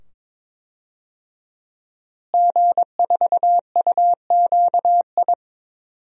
Morse code, heard as a keyed tone, about 22 wpm, 700 Hz. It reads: G4UQI
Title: DXCC-100 Call Signs - 5 Characters - Repeated 2x - 22wpm